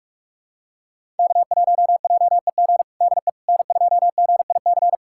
Morse code, heard as a keyed tone, about 46 wpm, 700 Hz. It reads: K1JEG DE N2GIC
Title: USA Callsign DE Callsign 45wpm